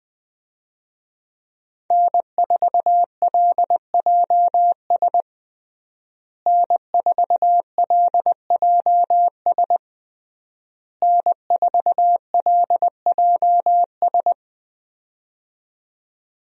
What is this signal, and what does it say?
Morse code, heard as a keyed tone, about 20 wpm, 700 Hz. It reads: N4LJS N4LJS N4LJS
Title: USA Call Signs - 5 Characters - Repeated 3x - 20wpm